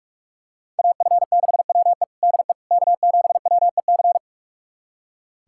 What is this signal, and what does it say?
Morse code, heard as a keyed tone, about 44 wpm, 700 Hz. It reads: AF6WE DE K7WEC
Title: USA Callsign DE Callsign - 5 Characters - 45wpm